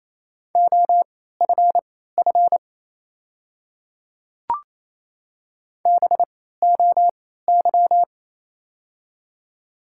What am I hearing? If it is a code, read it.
Morse code, heard as a keyed tone, about 28 wpm, spaced as for 16 wpm, 700 Hz. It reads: OFF BOY
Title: 3x Character Spacing - Top 200 Words - 28wpm